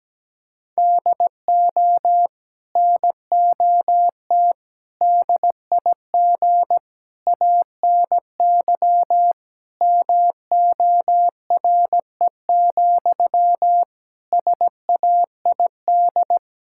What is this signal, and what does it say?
Morse code, heard as a keyed tone, about 17 wpm, 700 Hz. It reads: DO NOT DIG ANY MORE, SAID
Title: Sentences from Top 1600 Words 17wpm